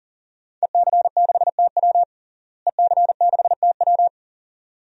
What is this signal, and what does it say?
Morse code, heard as a keyed tone, about 40 wpm, 700 Hz. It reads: EC6TW EC6TW
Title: DXCC-100 Call Signs - 5 Characters - Repeated 2x - 40wpm